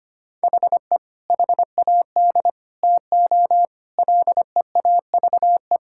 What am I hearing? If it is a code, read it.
Morse code, heard as a keyed tone, about 25 wpm, 700 Hz. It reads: HE HAD TO LEAVE
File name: practice / 25wpm / Sentences with 3-Word Phrases - 25wpm